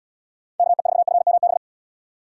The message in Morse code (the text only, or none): KE4FMB